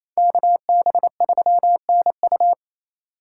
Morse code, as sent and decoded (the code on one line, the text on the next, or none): -.- -... ...-- -. ..-
KB3NU